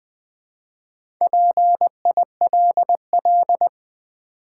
Morse code, heard as a keyed tone, about 20 wpm, 700 Hz. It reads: PILL
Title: Build A Word - 2 Letters At A Time - 4-Letter Words - 20wpm